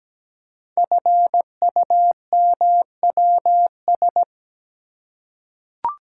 Morse code, heard as a keyed tone, about 17 wpm, 700 Hz. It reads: FUMWS
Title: Code Groups - 5 Letters - 17wpm